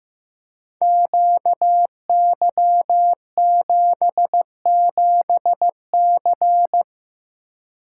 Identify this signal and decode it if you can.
Morse code, heard as a keyed tone, about 15 wpm, 700 Hz. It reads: QY77C
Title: Code Groups - 5 Letter:Numbers - 15wpm